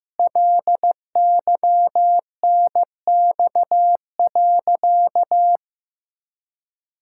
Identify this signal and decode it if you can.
Morse code, heard as a keyed tone, about 15 wpm, 700 Hz. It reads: LYNX.
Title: Pangram Challenge v2 - 15wpm